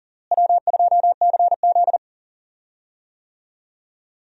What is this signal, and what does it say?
Morse code, heard as a keyed tone, about 38 wpm, 700 Hz. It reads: W2CZ
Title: USA Call Signs - 4 Characters - Repeated 2x - 40wpm